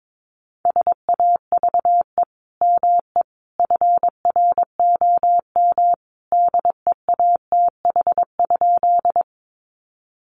Morse code, heard as a keyed tone, about 22 wpm, 700 Hz. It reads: SAVE ME FROM DEATH?
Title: Sentences from Top 1100 Words 22wpm